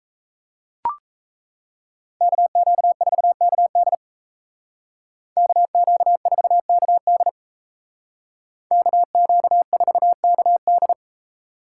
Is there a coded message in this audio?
Yes: Morse code from 2.2 to 10.9 s, heard as a keyed tone, about 37 wpm, 700 Hz. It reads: KQ4KD KQ4KD KQ4KD